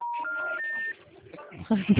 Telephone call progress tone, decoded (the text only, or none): special information tone